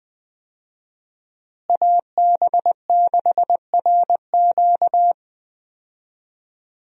Morse code, heard as a keyed tone, about 20 wpm, 700 Hz. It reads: AB6RQ
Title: USA Call Signs - 5 Characters - Repeated 3x - 20wpm